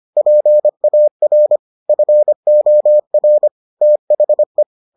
Morse code, heard as a keyed tone, about 25 wpm, 600 Hz. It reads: PAR FOR THE